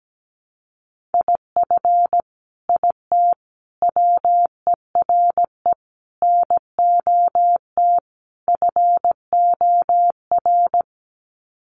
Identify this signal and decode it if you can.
Morse code, heard as a keyed tone, about 17 wpm, 700 Hz. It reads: IF IT WERE NOT FOR
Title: Sets of 5 Words 17wpm